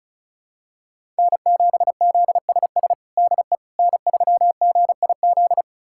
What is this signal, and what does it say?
Morse code, heard as a keyed tone, about 36 wpm, 700 Hz. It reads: N7ZSS DE N3GIZ
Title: USA Callsign DE Callsign - 5 Characters - 35wpm